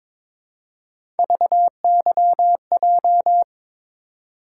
Morse code, heard as a keyed tone, about 22 wpm, 700 Hz. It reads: VYJ
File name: practice / 22wpm / Three Letter-Number Combinations - 22wpm